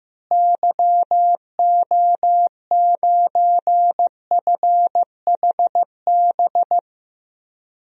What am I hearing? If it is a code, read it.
Morse code, heard as a keyed tone, about 15 wpm, 700 Hz. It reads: YO9FHB
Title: DXCC-100 Call Signs - 6 Characters - Repeated 2x - 15wpm